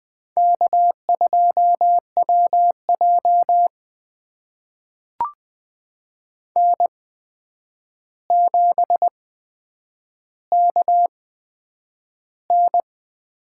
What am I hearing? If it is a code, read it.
Morse code, heard as a keyed tone, about 20 wpm, spaced as for 9 wpm, 700 Hz. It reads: K2WJ N7KN
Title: USA Call Signs - 4 Characters - 8x Character Spacing - 20wpm